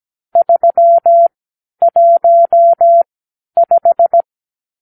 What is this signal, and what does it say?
Morse code, heard as a keyed tone, about 17 wpm, spaced as for 10 wpm, 680 Hz. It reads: 315
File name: arrl / Review (1-5)